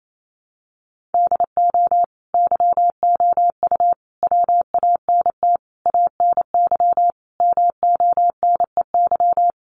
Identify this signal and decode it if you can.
Morse code, heard as a keyed tone, about 28 wpm, 700 Hz. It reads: DO YOU WANT ANY MONEY
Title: Sentences from Top 1400 Words 28wpm